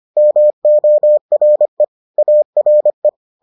Morse code, heard as a keyed tone, about 25 wpm, 600 Hz. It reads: MORE ARE